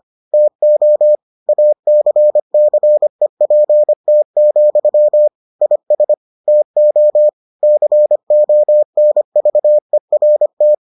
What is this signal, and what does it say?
Morse code, heard as a keyed tone, about 25 wpm, 600 Hz. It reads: TO ACCEPT, IS TO CONVERT